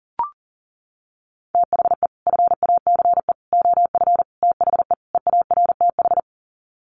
Morse code, heard as a keyed tone, about 40 wpm, 700 Hz. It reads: THE FACE OF THE EARTH
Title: Sets of 5 Words Encore 40wpm